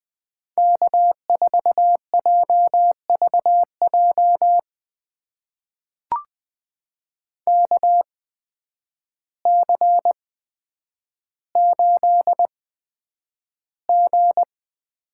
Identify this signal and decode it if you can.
Morse code, heard as a keyed tone, about 20 wpm, spaced as for 9 wpm, 700 Hz. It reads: K4JVJ KC8G
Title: USA Call Signs - 8x Character Spacing - 20wpm